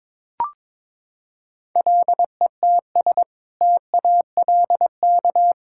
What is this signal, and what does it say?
Morse code, heard as a keyed tone, about 22 wpm, 700 Hz. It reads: LETS TALK
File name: practice / 22wpm / Sentences with 2-Word Phrases - 22wpm